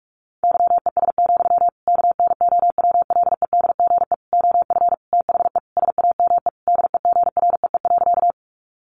Morse code, heard as a keyed tone, about 44 wpm, 700 Hz. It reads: YES, KNOWLEDGE OF THE SAME DEGREE.
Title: Sentences from Top 1500 Words 45wpm